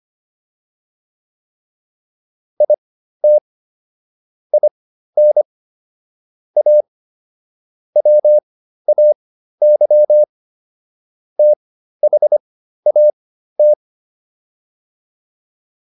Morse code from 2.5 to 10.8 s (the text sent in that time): IT IN A WAY